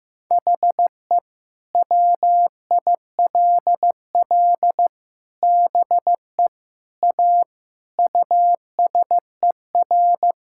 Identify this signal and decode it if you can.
Morse code, heard as a keyed tone, about 16 wpm, 700 Hz. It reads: HE WILL BE A USER
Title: Sentences from Top 1000 Words - Review Part 5 - 15wpm